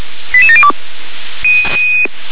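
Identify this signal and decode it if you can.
Five-tone selective call (ZVEI), heard as quick runs of five tones, tones of about 70 ms. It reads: 88702
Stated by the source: ZVEI five-tone selective call received with an SDR.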